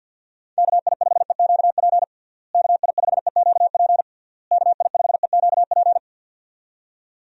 Morse code, heard as a keyed tone, about 47 wpm, 700 Hz. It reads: KI5EQP KI5EQP KI5EQP
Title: USA Call Signs - 6 Characters - Repeated 3x - 50wpm